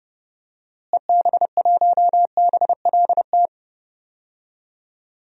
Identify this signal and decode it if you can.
Morse code, heard as a keyed tone, about 30 wpm, 700 Hz. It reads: EB1BLT
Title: DXCC-100 Call Signs - 6 Characters - Repeated 3x - 30wpm